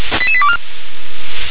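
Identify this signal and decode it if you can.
Five-tone selective call (ZVEI), heard as quick runs of five tones, tones of about 70 ms. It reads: 88824
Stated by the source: A ZVEI five-tone selective call, code 88832, received with an SDR.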